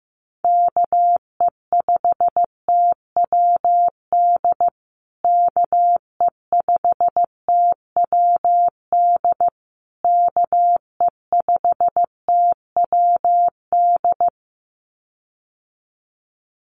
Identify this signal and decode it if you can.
Morse code, heard as a keyed tone, about 15 wpm, 700 Hz. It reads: KE5TWD KE5TWD KE5TWD
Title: USA Call Signs Repeated 3x 15wpm